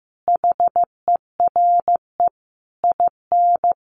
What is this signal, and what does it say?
Morse code, heard as a keyed tone, about 15 wpm, 700 Hz. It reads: HERE IN
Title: Sentences from Top 1100 Words 15wpm